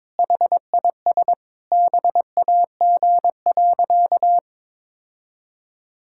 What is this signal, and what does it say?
Morse code, heard as a keyed tone, about 22 wpm, 700 Hz. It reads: HIS BAG.
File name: practice / 22wpm / Sentences with 3-Word Phrases - 22wpm